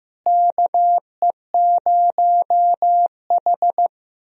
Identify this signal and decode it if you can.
Morse code, heard as a keyed tone, about 16 wpm, 700 Hz. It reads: KE0H
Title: USA Call Signs - 4 Characters - Repeated 2x - 15wpm